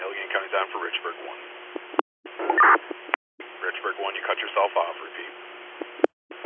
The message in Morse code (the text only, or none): none